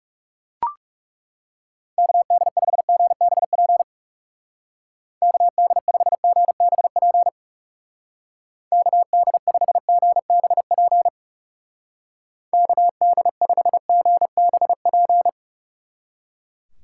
Morse code, heard as a keyed tone, about 36 wpm, 700 Hz. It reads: KD5GBP KD5GBP KD5GBP KD5GBP
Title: Speed Racing - USA 2x3 Call Signs 30wpm